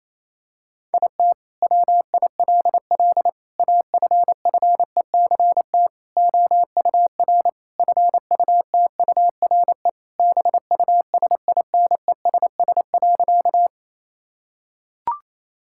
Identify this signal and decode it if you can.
Morse code, heard as a keyed tone, about 28 wpm, 700 Hz. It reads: IT WILL AFFECT OUR FUTURE BUSINESS.
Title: Sentences from Top 1000 Words - Review Part 6 - 28wpm